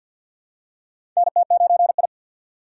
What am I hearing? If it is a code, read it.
Morse code, heard as a keyed tone, about 52 wpm, 690 Hz. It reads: NT9I
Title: USA Call Signs - 3x Character Spacing - 50wpm